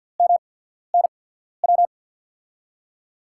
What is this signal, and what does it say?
Morse code, heard as a keyed tone, about 48 wpm, spaced as for 14 wpm, 700 Hz. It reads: MNW
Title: USA Call Signs - 6 Characters - 8x Character Spacing - 50wpm